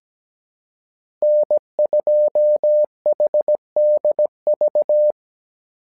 Morse code, heard as a keyed tone, about 17 wpm, 600 Hz. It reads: N2HDV
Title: SOTA QSO Practice 17wpm